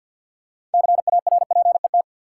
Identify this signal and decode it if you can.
Morse code, heard as a keyed tone, about 52 wpm, 700 Hz. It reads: CARPET
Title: Six Letter Words - 50wpm